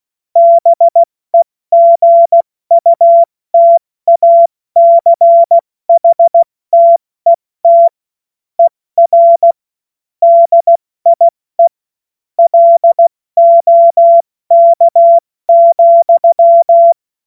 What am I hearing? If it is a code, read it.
Morse code, heard as a keyed tone, about 16 wpm, 680 Hz. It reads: BEGUTACHTET ER DIE LOK,
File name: Morsekurs / Witze / Witz 21. Juni 2022 70 F80